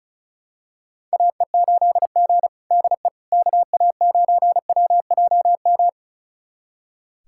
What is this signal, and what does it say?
Morse code, heard as a keyed tone, about 36 wpm, 700 Hz. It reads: AE8G DE KA9WJM
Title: USA Callsign DE Callsign 35wpm